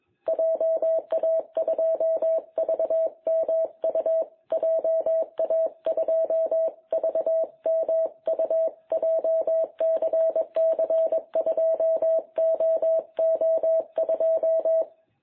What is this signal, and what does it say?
Morse code, heard as a keyed tone, about 21 wpm, 660 Hz. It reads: JA2VMU JA2VMU JCC2OO2
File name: jh8cbh / 2021cw